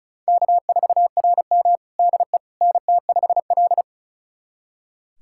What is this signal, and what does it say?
Morse code, heard as a keyed tone, about 36 wpm, 700 Hz. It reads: K4RM DE NT5L
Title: USA Callsign DE Callsign - 4 Characters - 35wpm